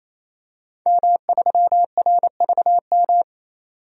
Morse code, heard as a keyed tone, about 28 wpm, 700 Hz. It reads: M3RVM
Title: DXCC-100 Call Signs - 5 Characters - 28wpm